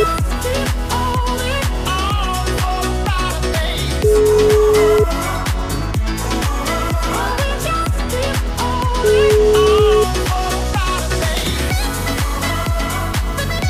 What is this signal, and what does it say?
Ringback tone (Europe, 425 Hz), with noise.